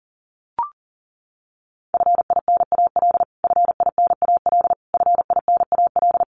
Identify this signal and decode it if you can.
Morse code, heard as a keyed tone, about 40 wpm, 700 Hz. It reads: FINAL FINAL FINAL
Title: Top 500 Words Repeated 40wpm